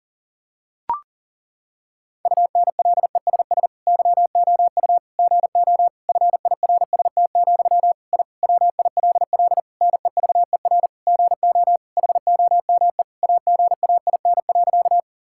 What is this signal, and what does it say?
Morse code, heard as a keyed tone, about 40 wpm, 700 Hz. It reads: UNLESS YOU GO FIRST, I WILL NEVER GO HOME AGAIN.
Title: Sentences from Top 1000 Words - Review Part 6 - 40wpm